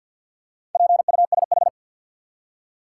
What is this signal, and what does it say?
Morse code, heard as a keyed tone, about 50 wpm, 700 Hz. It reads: PUSH
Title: Four Letter Words - 50wpm